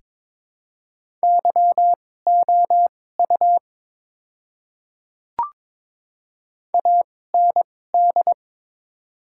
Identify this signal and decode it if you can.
Morse code, heard as a keyed tone, about 22 wpm, spaced as for 16 wpm, 700 Hz. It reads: YOU AND
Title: 2x Character Spacing - Top 200 Words - 22wpm